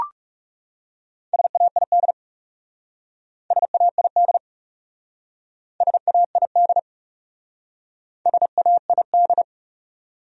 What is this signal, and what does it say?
Morse code, heard as a keyed tone, about 38 wpm, 700 Hz. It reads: SAID SAID SAID SAID